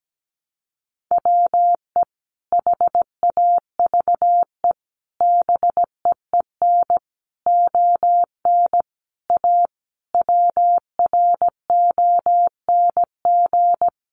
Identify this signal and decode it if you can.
Morse code, heard as a keyed tone, about 17 wpm, 700 Hz. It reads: WE HAVE BEEN ON A WRONG
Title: Sentences from Top 1100 Words 17wpm